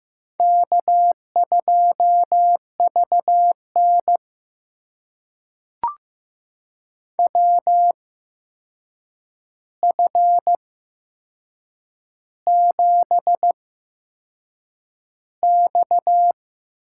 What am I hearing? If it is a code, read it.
Morse code, heard as a keyed tone, about 15 wpm, spaced as for 6 wpm, 700 Hz. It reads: K2VN WF7X
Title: USA Call Signs - 4 Characters - 8x Character Spacing - 15wpm